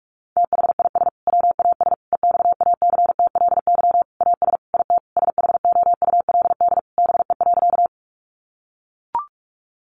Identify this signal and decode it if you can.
Morse code, heard as a keyed tone, about 44 wpm, 700 Hz. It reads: THIS WAS EXACTLY AS IT SHOULD BE.
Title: Sentences from Top 900 Words 45wpm